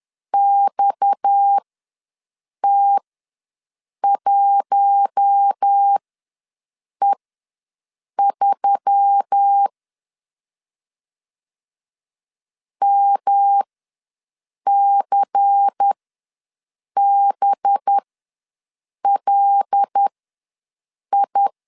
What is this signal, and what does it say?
Morse code, heard as a keyed tone, about 10 wpm, spaced as for 6 wpm, 800 Hz. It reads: XT1E3 MCBLI
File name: LEZIONI / 6 PKX0 / 6 PKX ws 10wpm 3706